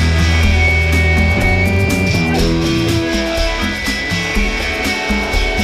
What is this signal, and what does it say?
Dial-up modem handshake.